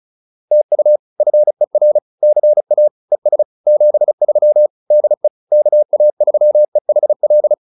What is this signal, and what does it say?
Morse code, heard as a keyed tone, about 34 wpm, 590 Hz. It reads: TU FER CA ES 73 DE KA3EHL
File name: POTA109 35